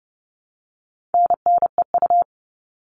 Morse code, heard as a keyed tone, about 30 wpm, 700 Hz. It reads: NNEU
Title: All Cut Numbers - 4-Digit 30wpm